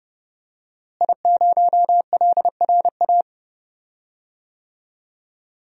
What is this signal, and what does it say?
Morse code, heard as a keyed tone, about 31 wpm, 700 Hz. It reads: I0LRA
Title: DXCC-100 Call Signs - 5 Characters - Repeated 3x - 30wpm